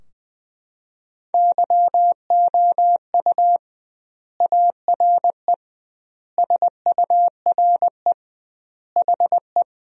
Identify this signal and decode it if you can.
Morse code, heard as a keyed tone, about 20 wpm, 700 Hz. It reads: YOU ARE SURE HE
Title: 2x Word-Spacing - Sentences from Top 300 Words - 20wpm